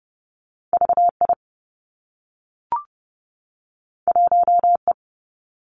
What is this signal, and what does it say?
Morse code, heard as a keyed tone, about 30 wpm, 700 Hz. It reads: V I 1 E
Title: Two Challenging Letter-Number Combinations 30wpm